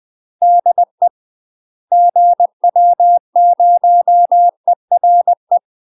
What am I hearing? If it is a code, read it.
Morse code, heard as a keyed tone, about 20 wpm, 700 Hz. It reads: DE GW0ERE
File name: UK MASTER QSOs TRAINING / 20WPM / UK MASTER QSOs 0045